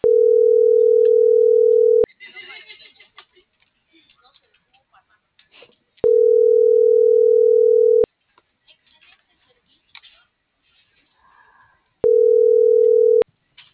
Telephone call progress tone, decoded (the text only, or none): ringback tone